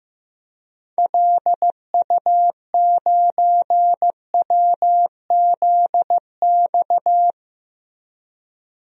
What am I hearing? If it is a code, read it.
Morse code, heard as a keyed tone, about 15 wpm, 700 Hz. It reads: LU9WZX
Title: DXCC-100 Call Signs - 6 Characters - Repeated 3x - 15wpm